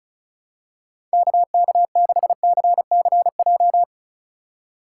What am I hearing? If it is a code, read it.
Morse code, heard as a keyed tone, about 36 wpm, 700 Hz. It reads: KK6CCJ
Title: USA Call Signs - 4x Character Spacing - 35wpm